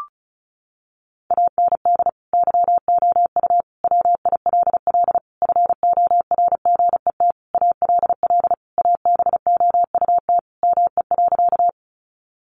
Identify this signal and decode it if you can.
Morse code, heard as a keyed tone, about 34 wpm, 700 Hz. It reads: AND YOU WILL FORGET ALL ABOUT ME.
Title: Sentences from Top 1200 Words 35wpm